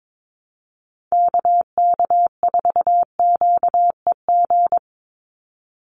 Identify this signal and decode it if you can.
Morse code, heard as a keyed tone, about 22 wpm, 700 Hz. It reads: KK4QEG
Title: USA Call Signs 22wpm